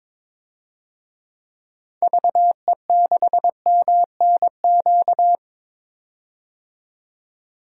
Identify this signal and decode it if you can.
Morse code, heard as a keyed tone, about 22 wpm, 700 Hz. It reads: VE6MNQ
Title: DXCC-100 Call Signs - 6 Characters - 22wpm